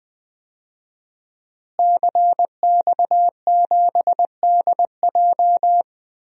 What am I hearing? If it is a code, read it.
Morse code, heard as a keyed tone, about 20 wpm, 700 Hz. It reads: CX7DJ
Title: DXCC-100 Call Signs - 5 Characters - Repeated 3x - 20wpm